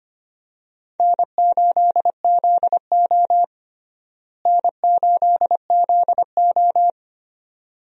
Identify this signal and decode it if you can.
Morse code, heard as a keyed tone, about 25 wpm, 700 Hz. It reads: N8ZO N8ZO
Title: USA Call Signs - 4 Characters - Repeated 2x - 25wpm